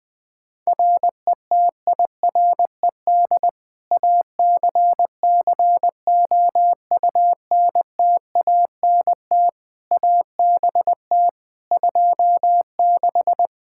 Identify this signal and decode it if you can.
Morse code, heard as a keyed tone, about 20 wpm, 700 Hz. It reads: RETIRED ACCOUNTANT ABT 26
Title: Practice QSO - Exchange 3 - More Info - 20wpm